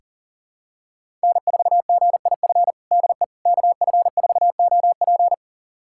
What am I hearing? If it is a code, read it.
Morse code, heard as a keyed tone, about 40 wpm, 700 Hz. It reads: N4GIF DE KF4OP
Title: USA Callsign DE Callsign - 5 Characters - 40wpm